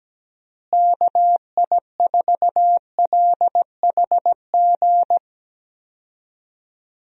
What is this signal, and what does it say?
Morse code, heard as a keyed tone, about 17 wpm, 700 Hz. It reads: KI4LHG
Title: USA Call Signs - 6 Characters - Repeated 2x - 17wpm